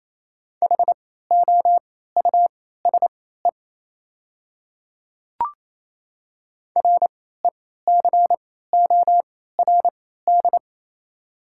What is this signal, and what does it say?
Morse code, heard as a keyed tone, about 28 wpm, spaced as for 16 wpm, 700 Hz. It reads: HOUSE RECORD
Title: 3x Character Spacing - Top 500 Words - 28wpm